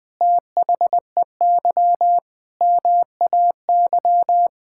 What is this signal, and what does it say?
Morse code, heard as a keyed tone, about 20 wpm, 700 Hz. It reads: THEY MAY
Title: Sets of 8 Words - 20wpm